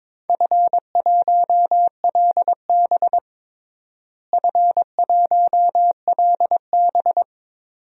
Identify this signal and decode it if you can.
Morse code, heard as a keyed tone, about 22 wpm, 700 Hz. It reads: F1LB F1LB
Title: DXCC-100 Call Signs - 4 Characters - Repeated 2x - 22wpm